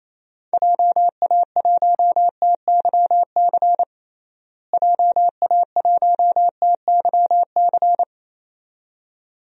Morse code, heard as a keyed tone, about 28 wpm, 700 Hz. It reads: JA1TYC JA1TYC
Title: DXCC-100 Call Signs - 6 Characters - Repeated 2x - 28wpm